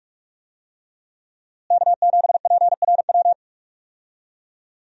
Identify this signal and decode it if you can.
Morse code, heard as a keyed tone, about 46 wpm, 700 Hz. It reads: K7PRW
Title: USA Call Signs - 5 Characters - Repeated 2x - 45wpm